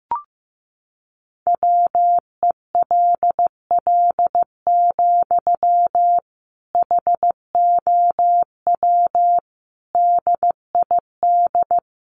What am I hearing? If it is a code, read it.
Morse code, heard as a keyed tone, about 15 wpm, 700 Hz. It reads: WELL, HOW DID
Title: Sentences from Top 200 Words 15wpm